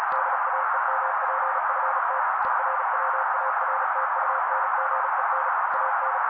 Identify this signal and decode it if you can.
Morse code, heard as a keyed tone, about 35 wpm, 530 Hz. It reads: DAEMPFT ERWARTUNGEN AN
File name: daily / 35wpm noise